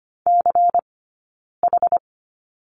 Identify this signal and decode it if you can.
Morse code, heard as a keyed tone, about 25 wpm, spaced as for 9 wpm, 690 Hz. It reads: CH
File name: course / 25:10wpm / 092 Lesson 16d - Letter c - Two Characters 25wpm